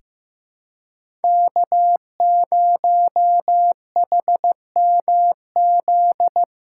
Morse code, heard as a keyed tone, about 15 wpm, 700 Hz. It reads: K0HMZ
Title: K1USN SST QSO Practice - 15wpm